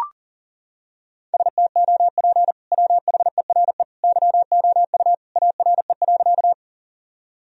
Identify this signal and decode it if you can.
Morse code, heard as a keyed tone, about 40 wpm, 700 Hz. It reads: STOP WHERE YOU ARE.
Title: Sentences from Top 1000 Words - Review Part 1 - 40wpm